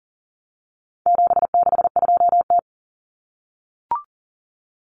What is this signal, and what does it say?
Morse code, heard as a keyed tone, about 40 wpm, 700 Hz. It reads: 762T
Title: Common Cut Numbers - 4-Digit 40wpm